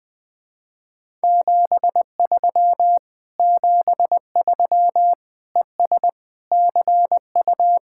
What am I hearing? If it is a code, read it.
Morse code, heard as a keyed tone, about 20 wpm, 700 Hz. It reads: 73 73 ES CU
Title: Speed-Racing Common QSO Phrases - 17wpm